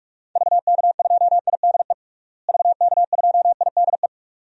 Morse code, heard as a keyed tone, about 46 wpm, 700 Hz. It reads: VK2IDE VK2IDE
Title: DXCC-100 Call Signs - 6 Characters - Repeated 2x - 45wpm